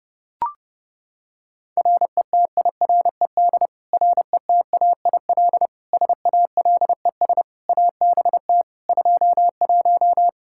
Morse code, heard as a keyed tone, about 30 wpm, 700 Hz. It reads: RETIRED RETAIL SALES ABT 21
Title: Practice QSO - Exchange 3 - More Info - 30wpm